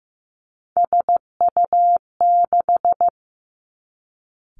Morse code, heard as a keyed tone, about 15 wpm, 700 Hz. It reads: SU6
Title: Three Challenging Letter-Number Combinations 15wpm